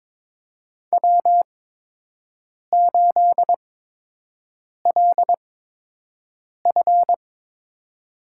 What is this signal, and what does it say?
Morse code, heard as a keyed tone, about 22 wpm, spaced as for 6 wpm, 700 Hz. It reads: W8LF
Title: USA Call Signs - 4 Characters - 8x Character Spacing - 22wpm